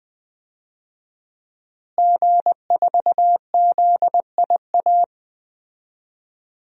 Morse code, heard as a keyed tone, about 20 wpm, 700 Hz. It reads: G4ZIA
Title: DXCC-100 Call Signs - 5 Characters - Repeated 3x - 20wpm